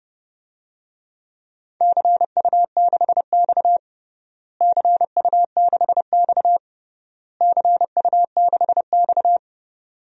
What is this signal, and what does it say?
Morse code, heard as a keyed tone, about 30 wpm, 700 Hz. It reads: CU6X CU6X CU6X
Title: DXCC-100 Call Signs - 4 Characters - Repeated 3x - 30wpm